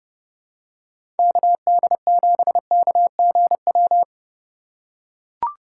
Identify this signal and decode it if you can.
Morse code, heard as a keyed tone, about 31 wpm, 700 Hz. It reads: KD7KGW
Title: USA Call Signs - 6 Characters - 4x Character Spacing - 30wpm